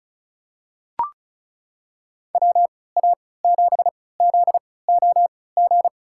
Morse code, heard as a keyed tone, about 36 wpm, spaced as for 20 wpm, 700 Hz. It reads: WA7ZOG
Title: USA Call Signs - 6 Characters - 3x Character Spacing - 35wpm